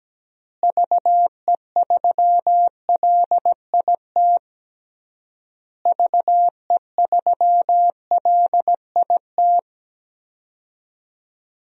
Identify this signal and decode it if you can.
Morse code, heard as a keyed tone, about 17 wpm, 700 Hz. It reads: VE3LIT VE3LIT
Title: DXCC-100 Call Signs - 6 Characters - Repeated 3x - 17wpm